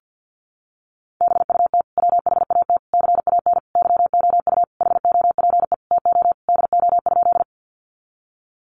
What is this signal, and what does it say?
Morse code, heard as a keyed tone, about 50 wpm, 700 Hz. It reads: BUT WHAT CAN YOU HOPE TO DO?
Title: Sentences from Top 1000 Words 50wpm